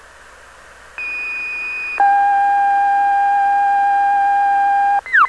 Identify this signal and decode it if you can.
Two-tone page, heard as a one-second tone, then a three-second tone, about 2500 then 800 Hz. It is heard from 1.0 to 5.0 s.
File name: plectron